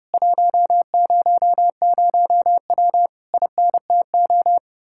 Morse code, heard as a keyed tone, about 33 wpm, 700 Hz. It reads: 100W INTO